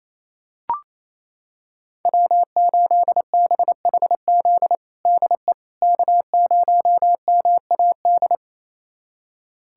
Morse code, heard as a keyed tone, about 28 wpm, 700 Hz. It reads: W8BHZ DE K0MAD
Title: USA Callsign DE Callsign - 5 Characters - 28wpm